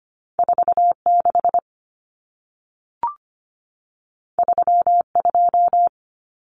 Morse code, heard as a keyed tone, about 25 wpm, 700 Hz. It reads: 46 32
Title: Numbers - 2-Digit 25wpm